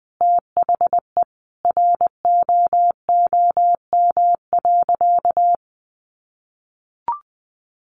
Morse code, heard as a keyed tone, about 19 wpm, 700 Hz. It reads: THE ROOM.
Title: Sentences from Top 500 Words 20wpm